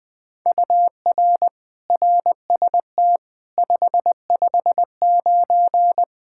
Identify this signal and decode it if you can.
Morse code, heard as a keyed tone, about 21 wpm, 700 Hz. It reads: UR RST 559
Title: Practice QSO - Exchange 1 - Beginning - 20wpm